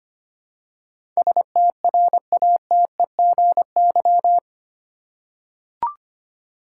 Morse code, heard as a keyed tone, about 25 wpm, 700 Hz. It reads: STRATEGY
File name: 8 Letter Words - First 3 Letters Given - 25wpm